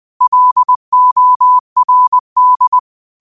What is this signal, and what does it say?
Morse code, heard as a keyed tone, about 20 wpm, 1000 Hz. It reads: LORD
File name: qotdfiles / 20wpm